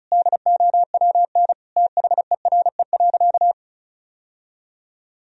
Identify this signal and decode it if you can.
Morse code, heard as a keyed tone, about 34 wpm, 700 Hz. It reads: DOWN THERE.
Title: Sentences from Top 1000 Words - Review Part 4 - 35wpm